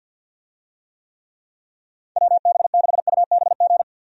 Morse code, heard as a keyed tone, about 49 wpm, 700 Hz. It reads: WB6UBG
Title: USA Call Signs - 6 Characters - 50wpm